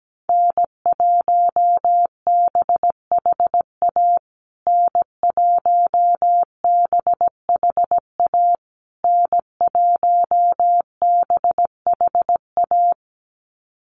Morse code, heard as a keyed tone, about 17 wpm, 700 Hz. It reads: N1BHA N1BHA N1BHA
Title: USA Call Signs Repeated 3x 17wpm